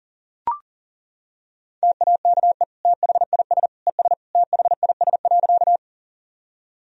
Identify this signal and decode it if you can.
Morse code, heard as a keyed tone, about 40 wpm, 700 Hz. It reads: TAKE THIS ES THIS.
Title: Sentences from Top 200 Words and Abbreviations 40wpm